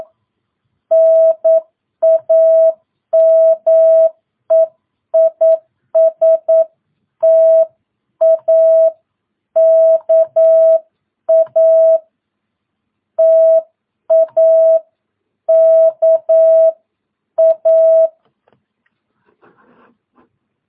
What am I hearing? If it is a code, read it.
Morse code, heard as a keyed tone, about 9 wpm, 650 Hz. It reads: NAME IS TAKA TAKA
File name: jh8cbh / 2022cw